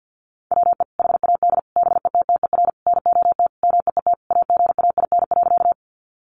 Morse code, heard as a keyed tone, about 50 wpm, 700 Hz. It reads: WE HAD BETTER NOT MEET AGAIN.
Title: Sentences from Top 1100 Words 50wpm